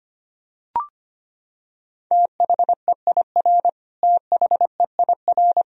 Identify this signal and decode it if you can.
Morse code, heard as a keyed tone, about 25 wpm, 700 Hz. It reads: THEIR THEIR
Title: Top 100 Words Repeated 25wpm